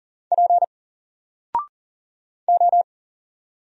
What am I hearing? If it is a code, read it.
Morse code, heard as a keyed tone, about 40 wpm, 700 Hz. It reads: P O
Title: Single Character (Letter:Number:Punctuation:Prosign) Alt 40wpm